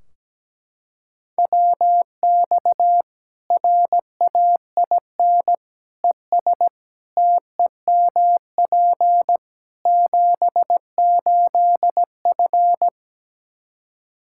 Morse code, heard as a keyed tone, about 17 wpm, 700 Hz. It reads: WX RAIN ES TEMP 78F
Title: K1USN SST QSO Practice 17wpm